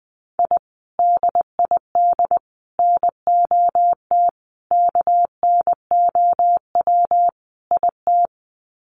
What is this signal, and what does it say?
Morse code, heard as a keyed tone, about 20 wpm, 700 Hz. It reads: I DID NOT KNOW IT
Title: Sets of 5 Words Encore2 20wpm